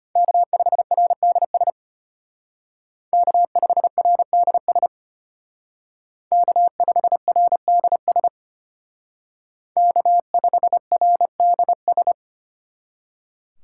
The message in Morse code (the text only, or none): K5RDS K5RDS K5RDS K5RDS